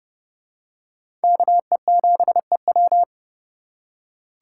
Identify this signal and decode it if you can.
Morse code, heard as a keyed tone, about 30 wpm, 700 Hz. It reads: KE7EW